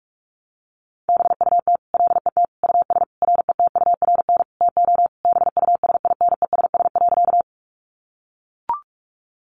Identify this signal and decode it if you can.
Morse code, heard as a keyed tone, about 44 wpm, 700 Hz. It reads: BUT LET US RETURN TO BUSINESS.
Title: Sentences from Top 800 Words 45wpm